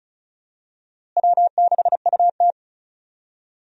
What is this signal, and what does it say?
Morse code, heard as a keyed tone, about 34 wpm, 700 Hz. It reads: W6UT
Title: USA Call Signs - 4 Characters - Repeated 2x - 35wpm